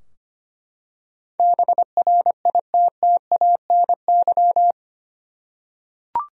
Morse code, heard as a keyed tone, about 25 wpm, 700 Hz. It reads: BRITTANY
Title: USA Common Names - 25wpm